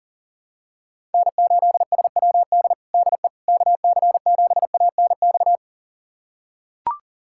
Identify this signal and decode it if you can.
Morse code, heard as a keyed tone, about 40 wpm, 700 Hz. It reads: N8SWD DE KC7ANX
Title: USA Callsign DE Callsign 40wpm